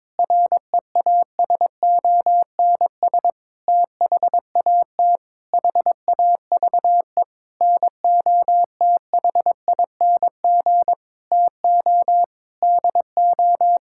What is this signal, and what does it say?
Morse code, heard as a keyed tone, about 22 wpm, 700 Hz. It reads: REASONS THAT HAVE NOTHING TO DO
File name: practice / 22wpm / Sentences with 8-Word Phrases - 22wpm